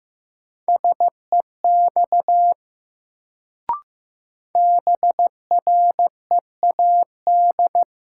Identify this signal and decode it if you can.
Morse code, heard as a keyed tone, about 15 wpm, 700 Hz. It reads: SEX BREAD
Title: Sentences from Top 1600 Words - New Word Drill - 15wpm